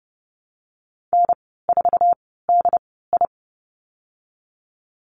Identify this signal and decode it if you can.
Morse code, heard as a keyed tone, about 30 wpm, spaced as for 17 wpm, 700 Hz. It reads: N4DI
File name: Lesson 19g - Letter w - Call Signs 30wpm